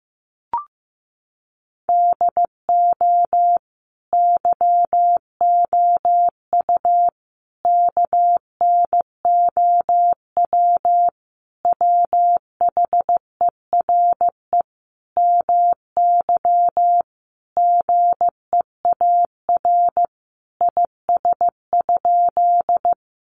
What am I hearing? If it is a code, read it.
Morse code, heard as a keyed tone, about 15 wpm, 700 Hz. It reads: DO YOU KNOW WHERE MY GEAR IS?
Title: Sentences from Top 1400 Words 15wpm